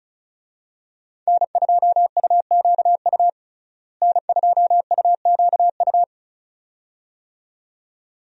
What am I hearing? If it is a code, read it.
Morse code, heard as a keyed tone, about 35 wpm, 700 Hz. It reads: N2UQU N2UQU
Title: USA Call Signs - 5 Characters - Repeated 2x - 35wpm